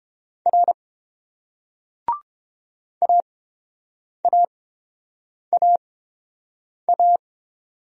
Morse code, heard as a keyed tone, about 30 wpm, spaced as for 7 wpm, 690 Hz. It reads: R AAAA